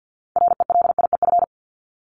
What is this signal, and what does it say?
Morse code, heard as a keyed tone, about 50 wpm, 700 Hz. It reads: RELIEF
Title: Top 1501-1750 Words 50wpm